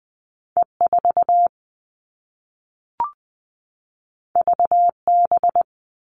Morse code, heard as a keyed tone, about 20 wpm, 700 Hz. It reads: E4 VB
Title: All Cut Numbers - 2-Digit 20wpm